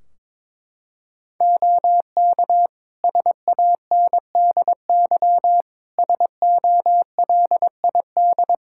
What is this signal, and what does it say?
Morse code, heard as a keyed tone, about 22 wpm, 700 Hz. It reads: OK SANDY SOLID
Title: Practice QSO - Exchange 3 - More Info - 22wpm